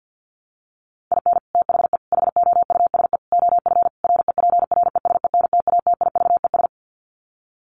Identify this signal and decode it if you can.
Morse code, heard as a keyed tone, about 50 wpm, 700 Hz. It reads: IN THE HOUSE OF REPRESENTATIVES
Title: Sets of 5 Words Encore 50wpm